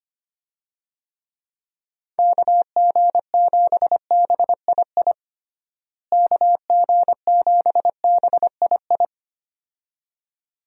Morse code, heard as a keyed tone, about 25 wpm, 700 Hz. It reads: KG7BII KG7BII
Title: USA Call Signs - 6 Characters - Repeated 2x - 25wpm